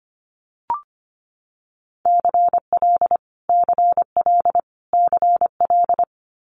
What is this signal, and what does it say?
Morse code, heard as a keyed tone, about 25 wpm, 700 Hz. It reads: CL CL CL
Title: Multi-letter Phonemes Repeated 3x 25wpm